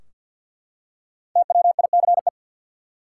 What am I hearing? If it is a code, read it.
Morse code, heard as a keyed tone, about 52 wpm, 700 Hz. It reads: TWICE
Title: Five Letter Words - 50wpm